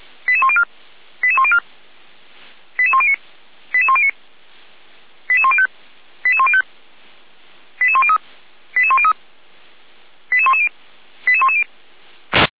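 Five-tone selective call (ZVEI), heard as quick runs of five tones, tones of about 70 ms. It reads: 88104 88104 88109 88109 88105 88105 88103 88103 88110 88110
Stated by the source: ZVEI five-tone selective call received by SDR, from audio.f85360.feuerwehr-orlamuende.org (Feuerwehr Orlamünde fire brigade).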